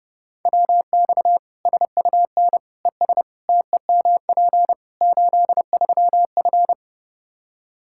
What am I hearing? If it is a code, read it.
Morse code, heard as a keyed tone, about 30 wpm, 700 Hz. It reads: WX SUN ES TEMP 83F
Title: K1USN SST QSO Practice 30wpm